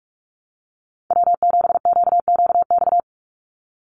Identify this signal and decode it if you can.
Morse code, heard as a keyed tone, about 44 wpm, 700 Hz. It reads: W7QQX
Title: USA Call Signs 45wpm